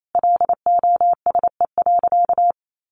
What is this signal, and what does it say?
Morse code, heard as a keyed tone, about 28 wpm, 700 Hz. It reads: LOSE.
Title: Sentences from Top 1100 Words 28wpm